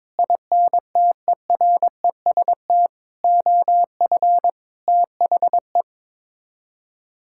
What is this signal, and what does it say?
Morse code, heard as a keyed tone, about 22 wpm, 700 Hz. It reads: INTEREST OF THE
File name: practice / 22wpm / Sets of 6 Words - 22wpm